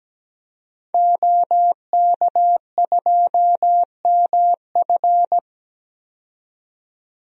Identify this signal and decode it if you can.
Morse code, heard as a keyed tone, about 17 wpm, 700 Hz. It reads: OK2MF
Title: DXCC-100 Call Signs - 5 Characters - 17wpm